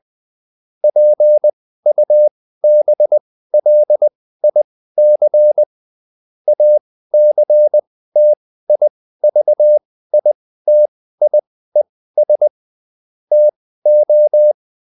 Morse code, heard as a keyed tone, about 20 wpm, spaced as for 14 wpm, 600 Hz. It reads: PUBLIC ACTIVITIES TO